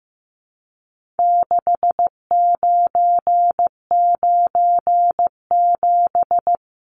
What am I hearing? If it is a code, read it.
Morse code, heard as a keyed tone, about 15 wpm, 700 Hz. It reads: 6997
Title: Numbers - 4-Digit 15wpm